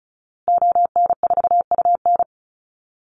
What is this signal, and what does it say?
Morse code, heard as a keyed tone, about 34 wpm, 700 Hz. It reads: ON4UN
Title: Call Signs Repeated 2x - 35wpm